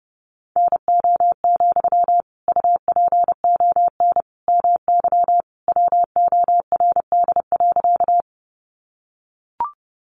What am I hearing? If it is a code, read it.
Morse code, heard as a keyed tone, about 30 wpm, 700 Hz. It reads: NO, UPON MY WORD.
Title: Sentences from Top 1100 Words 30wpm